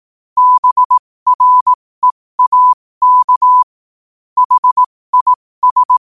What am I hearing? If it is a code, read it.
Morse code, heard as a keyed tone, about 18 wpm, spaced as for 15 wpm, 1000 Hz. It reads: BREAK HIS
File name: qotdfiles / 15wpm